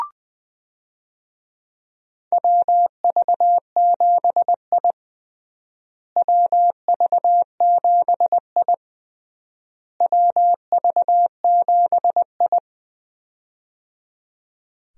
Morse code, heard as a keyed tone, about 20 wpm, 700 Hz. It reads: WV7I WV7I WV7I
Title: USA Call Signs - 4 Characters - Repeated 3x - 20wpm